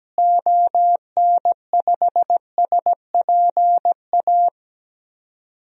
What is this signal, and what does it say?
Morse code, heard as a keyed tone, about 17 wpm, 700 Hz. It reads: ON5SPA
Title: DXCC-100 Call Signs - 6 Characters - Repeated 2x - 17wpm